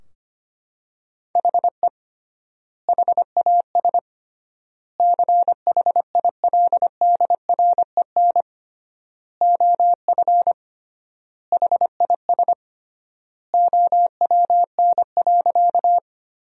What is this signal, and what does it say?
Morse code, heard as a keyed tone, about 25 wpm, 700 Hz. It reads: HE HAS CHILDREN OF HIS OWN.
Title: 3x Word-Spacing - Sentences from Top 300 Words - 25wpm